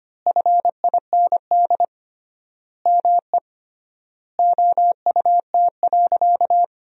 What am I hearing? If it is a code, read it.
Morse code, heard as a keyed tone, about 25 wpm, 700 Hz. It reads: FIND ME OUT.
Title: 3x Word-Spacing - Sentences from Top 500 Words - 25wpm